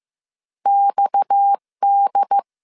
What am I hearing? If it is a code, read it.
Morse code, heard as a keyed tone, about 15 wpm, 800 Hz. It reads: XD
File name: CW / DET / 1 9 1500C S B 14wpm 3302